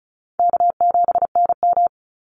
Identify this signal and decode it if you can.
Morse code, heard as a keyed tone, about 34 wpm, 700 Hz. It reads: K7NM
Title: Call Signs Repeated 2x - 35wpm